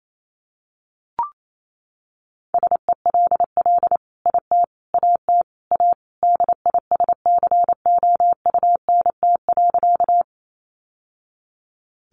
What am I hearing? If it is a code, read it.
Morse code, heard as a keyed tone, about 28 wpm, 700 Hz. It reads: SELL IT AT A DISCOUNT.
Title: Sentences from Top 1200 Words 28wpm